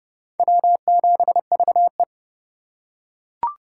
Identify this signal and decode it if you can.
Morse code, heard as a keyed tone, about 24 wpm, 700 Hz. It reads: W7VE
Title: Speed Racing - USA 1x2 Call Signs 20wpm